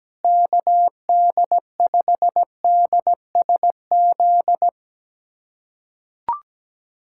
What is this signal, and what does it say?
Morse code, heard as a keyed tone, about 17 wpm, 700 Hz. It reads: KD5DSZ
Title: USA Callsign DE Callsign 17wpm